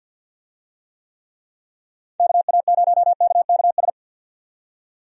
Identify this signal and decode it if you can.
Morse code, heard as a keyed tone, about 51 wpm, 700 Hz. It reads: KA0KKS
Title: USA Call Signs - 6 Characters - 50wpm